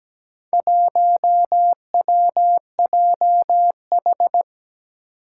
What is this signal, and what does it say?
Morse code, heard as a keyed tone, about 17 wpm, 700 Hz. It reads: 1WJH
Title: Four Challenging Letter-Number Combinations 17wpm